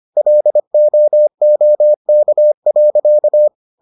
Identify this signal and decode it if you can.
Morse code, heard as a keyed tone, about 25 wpm, 600 Hz. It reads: LOOK.